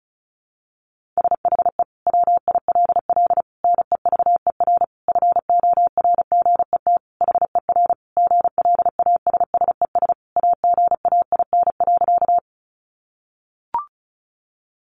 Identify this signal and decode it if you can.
Morse code, heard as a keyed tone, about 34 wpm, 700 Hz. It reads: SHE WILL NEVER FORGET HER GLASSES AGAIN.
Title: Additional Easy Phrases 35wpm